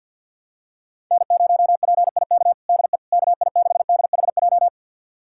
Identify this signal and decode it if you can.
Morse code, heard as a keyed tone, about 47 wpm, 700 Hz. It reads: N0PIK DE KI6DHJ
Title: USA Callsign DE Callsign 50wpm